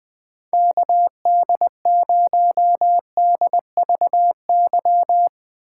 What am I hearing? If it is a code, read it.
Morse code, heard as a keyed tone, about 20 wpm, 700 Hz. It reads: KD0DVY